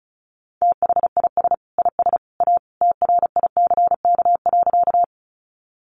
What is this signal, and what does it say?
Morse code, heard as a keyed tone, about 34 wpm, 700 Hz. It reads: THIS IS A TRICK.
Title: Sentences from Top 1500 Words 35wpm